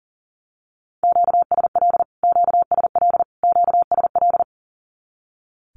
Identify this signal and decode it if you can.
Morse code, heard as a keyed tone, about 40 wpm, 700 Hz. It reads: QSL QSL QSL
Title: Q-Codes Repeated 3x 40wpm